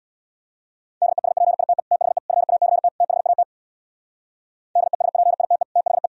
Morse code, heard as a keyed tone, about 64 wpm, 700 Hz. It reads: DESPITE THE FACT THAT DESPITE THE